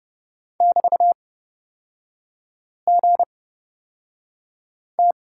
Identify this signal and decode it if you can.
Morse code, heard as a keyed tone, about 30 wpm, spaced as for 5 wpm, 700 Hz. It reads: <BT>GT
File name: Lesson 36e - BT - Three Characters 30wpm